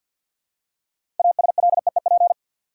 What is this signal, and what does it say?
Morse code, heard as a keyed tone, about 53 wpm, 700 Hz. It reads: ASLEEP
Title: Six Letter Words - 50wpm